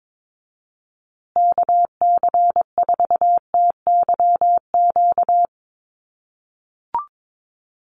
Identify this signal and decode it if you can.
Morse code, heard as a keyed tone, about 22 wpm, 700 Hz. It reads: KC4TYQ
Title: USA Call Signs Repeated 3x 22wpm